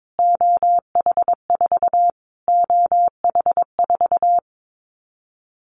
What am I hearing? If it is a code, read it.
Morse code, heard as a keyed tone, about 22 wpm, 700 Hz. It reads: OH4 OH4
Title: Callsign Prefixes for Top 25 DXCC - Repeated 3x - 22wpm